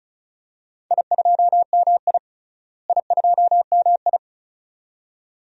Morse code, heard as a keyed tone, about 36 wpm, 700 Hz. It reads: I2MI I2MI
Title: DXCC-100 Call Signs - 4 Characters - Repeated 2x - 35wpm